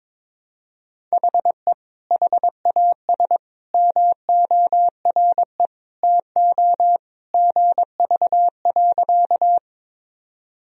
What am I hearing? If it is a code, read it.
Morse code, heard as a keyed tone, about 22 wpm, 700 Hz. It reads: HE HAS MORE TO GV.
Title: Sentences from Top 200 Words and Abbreviations 22wpm